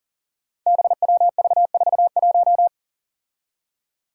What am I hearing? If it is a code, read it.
Morse code, heard as a keyed tone, about 40 wpm, 700 Hz. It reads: BWV41
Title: Five Challenging Letter-Number Combinations 40wpm